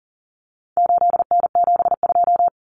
Morse code, heard as a keyed tone, about 40 wpm, 700 Hz. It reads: 8N72
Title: Common Cut Numbers - 4-Digit 40wpm